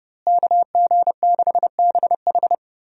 Morse code, heard as a keyed tone, about 30 wpm, 700 Hz. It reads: KG6BH
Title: USA Call Signs - 5 Characters - Repeated 2x - 30wpm